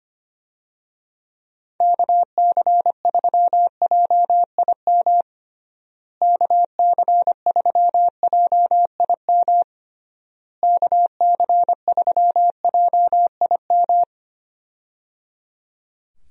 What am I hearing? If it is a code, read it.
Morse code, heard as a keyed tone, about 25 wpm, 700 Hz. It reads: KC3JIM KC3JIM KC3JIM
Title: USA Call Signs - 6 Characters - Repeated 3x - 25wpm